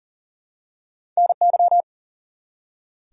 Morse code, heard as a keyed tone, about 41 wpm, 700 Hz. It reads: NY
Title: US States and Canadian Provinces 40wpm